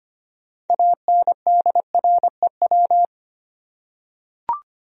Morse code, heard as a keyed tone, about 25 wpm, 700 Hz. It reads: ANDREW
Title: USA Common Names - 25wpm